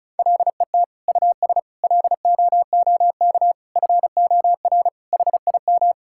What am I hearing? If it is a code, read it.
Morse code, heard as a keyed tone, about 36 wpm, 700 Hz. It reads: LET US LOOK FOR HIM
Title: Sentences from Top 1600 Words 35wpm